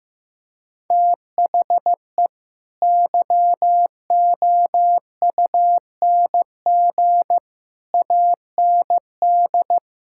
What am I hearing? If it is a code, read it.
Morse code, heard as a keyed tone, about 16 wpm, 700 Hz. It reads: THE YOUNG AND
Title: Sentences from Top 1000 Words - Review Part 5 - 15wpm